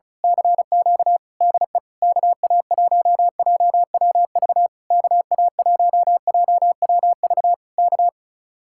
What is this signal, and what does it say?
Morse code, heard as a keyed tone, about 33 wpm, 700 Hz. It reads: CQ DE KA1JWV KA1JWV K
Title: Speed-Racing Common QSO Phrases - 35wpm